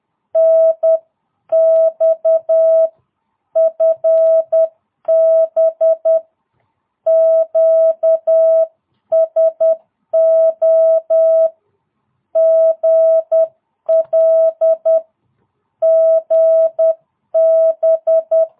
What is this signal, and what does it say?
Morse code, heard as a keyed tone, about 10 wpm, 650 Hz. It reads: NX FB QSO GL GB
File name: jh8cbh / 2022cw